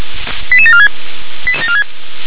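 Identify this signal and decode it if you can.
Five-tone selective call (ZVEI), heard as quick runs of five tones, tones of about 70 ms. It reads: 88746
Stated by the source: ZVEI five-tone selective call received on SDR.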